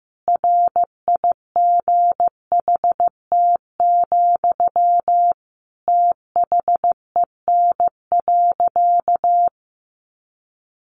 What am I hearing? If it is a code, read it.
Morse code, heard as a keyed tone, about 15 wpm, 700 Hz. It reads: RIGHT, THEN.
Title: Sentences from Top 200 Words 15wpm